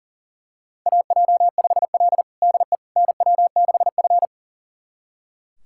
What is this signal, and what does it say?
Morse code, heard as a keyed tone, about 37 wpm, 700 Hz. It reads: AJ5L DE NW6F
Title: USA Callsign DE Callsign - 4 Characters - 40wpm